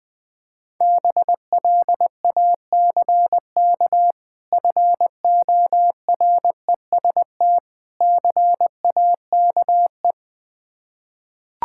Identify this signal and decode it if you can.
Morse code, heard as a keyed tone, about 20 wpm, 700 Hz. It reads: BLACK FOREST CAKE
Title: Difficult Words with Hints - 20wpm